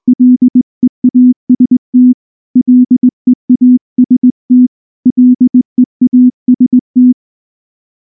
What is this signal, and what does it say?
Morse code, heard as a keyed tone, about 19 wpm, 250 Hz. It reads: LEAST LEAST LEAST
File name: morse / morse-training-20WPM-9